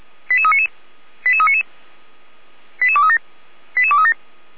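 Five-tone selective call (ZVEI), heard as quick runs of five tones, tones of about 70 ms. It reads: 88399 88399 88237 88237
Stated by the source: Five-tone ZVEI selective call received by SDR, from feuerwehr-orlamuende fire brigade server.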